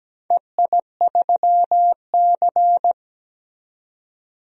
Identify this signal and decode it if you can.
Morse code, heard as a keyed tone, about 17 wpm, 700 Hz. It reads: EI3C
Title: DXCC-100 Call Signs - 4 Characters - 17wpm